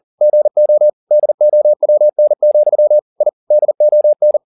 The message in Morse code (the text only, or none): GO DOWN, I DON